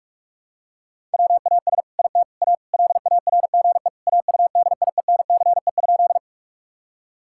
Morse code, heard as a keyed tone, about 46 wpm, 700 Hz. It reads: WAS IT A LARGE AUDIENCE?
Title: Sentences from Top 1000 Words - Review Part 5 - 45wpm